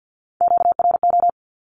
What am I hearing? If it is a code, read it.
Morse code, heard as a keyed tone, about 48 wpm, 700 Hz. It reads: QRO
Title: Q-Codes 50wpm